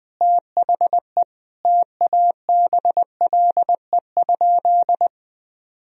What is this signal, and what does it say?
Morse code, heard as a keyed tone, about 21 wpm, 700 Hz. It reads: THE TABLE?
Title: Sentences from Top 1000 Words - Review Part 4 - 20wpm